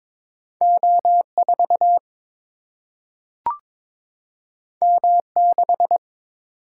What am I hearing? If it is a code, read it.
Morse code, heard as a keyed tone, about 22 wpm, 700 Hz. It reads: O4 M6
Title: Single-Letter Single-Number 22wpm